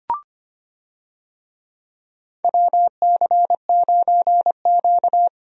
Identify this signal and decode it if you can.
Morse code, heard as a keyed tone, about 25 wpm, 700 Hz. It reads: WC9Q
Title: USA Call Signs - 4 Characters - 25wpm